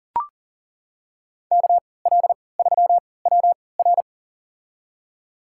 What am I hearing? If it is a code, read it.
Morse code, heard as a keyed tone, about 40 wpm, spaced as for 23 wpm, 700 Hz. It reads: KL3WR
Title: USA Call Signs - 5 Characters - 3x Character Spacing - 40wpm